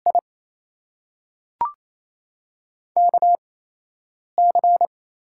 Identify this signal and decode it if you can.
Morse code, heard as a keyed tone, about 28 wpm, spaced as for 8 wpm, 700 Hz. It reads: I KC
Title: USA Call Signs - 6 Characters - 8x Character Spacing - 28wpm